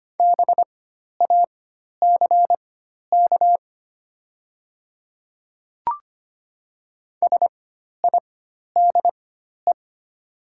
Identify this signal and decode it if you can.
Morse code, heard as a keyed tone, about 25 wpm, spaced as for 12 wpm, 700 Hz. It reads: BACK SIDE
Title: 4x Character Spacing - Top 200 Words - 25wpm